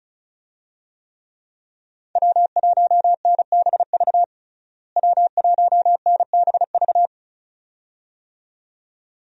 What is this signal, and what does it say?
Morse code, heard as a keyed tone, about 36 wpm, 700 Hz. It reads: W1NBV W1NBV
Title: USA Call Signs - 5 Characters - Repeated 2x - 35wpm